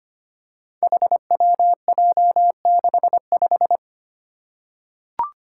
Morse code, heard as a keyed tone, about 25 wpm, 700 Hz. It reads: HWJ65
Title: Five Challenging Letter-Number Combinations 25wpm